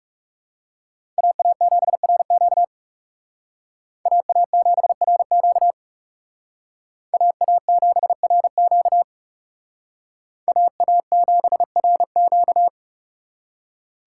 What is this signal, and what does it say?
Morse code, heard as a keyed tone, about 34 wpm, 700 Hz. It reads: AA7RQ AA7RQ AA7RQ AA7RQ
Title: Speed Racing - USA 2x2 Call Signs 30wpm